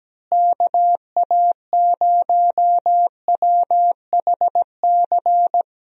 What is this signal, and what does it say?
Morse code, heard as a keyed tone, about 17 wpm, 700 Hz. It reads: KA0WHC
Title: USA Call Signs - 6 Characters - Repeated 2x - 17wpm